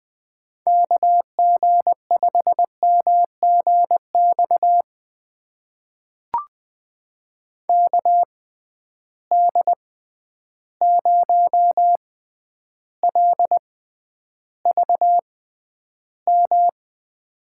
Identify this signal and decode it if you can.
Morse code, heard as a keyed tone, about 20 wpm, spaced as for 10 wpm, 700 Hz. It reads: KG5MGX KD0LVM
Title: USA Call Signs - 6 Characters - 6x Character Spacing - 20wpm